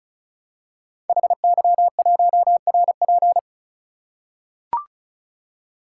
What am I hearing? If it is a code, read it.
Morse code, heard as a keyed tone, about 36 wpm, 690 Hz. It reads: HY1RP
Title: Code Groups - 5 Letter:Numbers - 35wpm